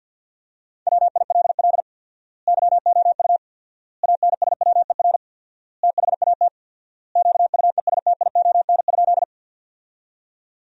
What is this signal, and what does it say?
Morse code, heard as a keyed tone, about 50 wpm, 700 Hz. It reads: WILL YOU ANSWER THAT QUESTION?
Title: 4x Word-Spacing - Sentences from Top 400 Words - 50wpm